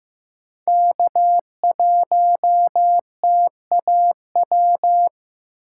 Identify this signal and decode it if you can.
Morse code, heard as a keyed tone, about 15 wpm, 700 Hz. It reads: K1TAW
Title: USA Call Signs - 5 Characters - 6x Character Spacing - 15wpm